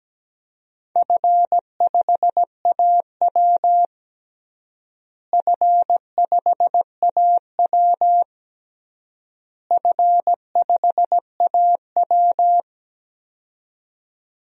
Morse code, heard as a keyed tone, about 17 wpm, 700 Hz. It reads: F5AW F5AW F5AW
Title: DXCC-100 Call Signs - 4 Characters - Repeated 3x - 17wpm